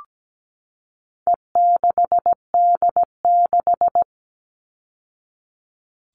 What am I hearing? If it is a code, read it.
Morse code, heard as a keyed tone, about 17 wpm, 700 Hz. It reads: E6D6
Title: All Cut Numbers - 4-Digit 17wpm